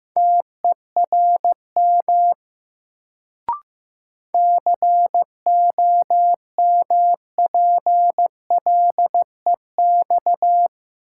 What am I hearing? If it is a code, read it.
Morse code, heard as a keyed tone, about 15 wpm, 700 Hz. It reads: TERM COMPLEX
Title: Sentences from Top 900 Words - New Word Drill - 15wpm